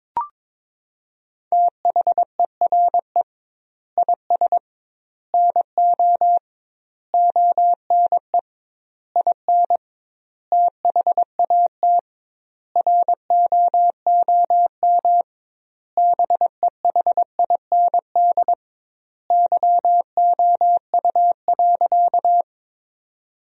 Morse code, heard as a keyed tone, about 22 wpm, 700 Hz. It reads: THERE IS NO ONE IN THAT ROOM BEHIND YOU.
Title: 2x Word-Spacing - Sentences from Top 500 Words - 22wpm